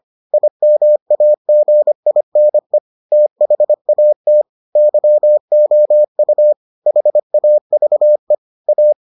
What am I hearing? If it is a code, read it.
Morse code, heard as a keyed tone, about 25 wpm, 600 Hz. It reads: IMAGINE THAT YOU HAVE A